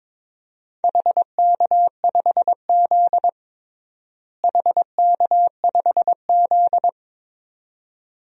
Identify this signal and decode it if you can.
Morse code, heard as a keyed tone, about 22 wpm, 700 Hz. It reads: HK5Z HK5Z
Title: DXCC-100 Call Signs - 4 Characters - Repeated 2x - 22wpm